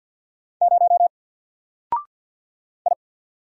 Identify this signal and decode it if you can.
Morse code, heard as a keyed tone, about 48 wpm, 700 Hz. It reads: 0 I
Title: Single Character (Letter:Number:Punctuation:Prosign) 50wpm